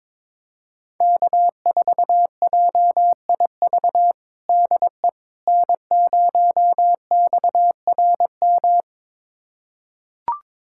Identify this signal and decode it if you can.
Morse code, heard as a keyed tone, about 22 wpm, 700 Hz. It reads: K4JIV DE N0XRM
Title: USA Callsign DE Callsign - 5 Characters - 22wpm